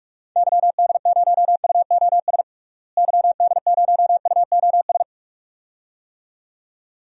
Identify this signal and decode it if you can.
Morse code, heard as a keyed tone, about 42 wpm, 700 Hz. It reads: YD0UOS YD0UOS
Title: DXCC-100 Call Signs - 6 Characters - Repeated 2x - 45wpm